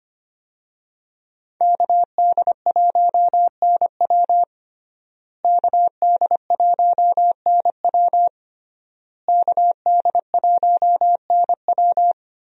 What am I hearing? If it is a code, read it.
Morse code, heard as a keyed tone, about 25 wpm, 700 Hz. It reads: KD1NW KD1NW KD1NW
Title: USA Call Signs - 5 Characters - Repeated 3x - 25wpm